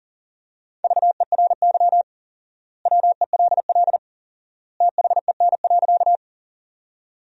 Morse code, heard as a keyed tone, about 40 wpm, 700 Hz. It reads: VERY WELL THEN.
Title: 4x Word-Spacing - Sentences from Top 200 Words - 40wpm